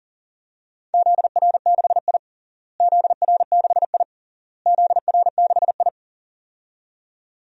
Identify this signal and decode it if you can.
Morse code, heard as a keyed tone, about 40 wpm, 700 Hz. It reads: ZR6I ZR6I ZR6I
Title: DXCC-100 Call Signs - 4 Characters - Repeated 3x - 40wpm